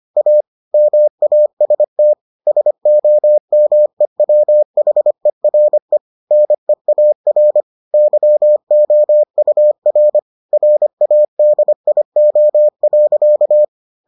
Morse code, heard as a keyed tone, about 25 wpm, 600 Hz. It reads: A MAST SOMEWHERE NEAR YOUR RADIO.